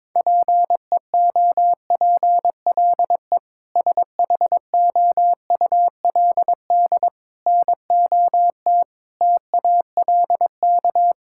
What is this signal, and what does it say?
Morse code, heard as a keyed tone, about 22 wpm, 700 Hz. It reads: PEOPLE SHOULD NOT TALK
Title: Sentences from Top 1000 Words - Review Part 6 - 22wpm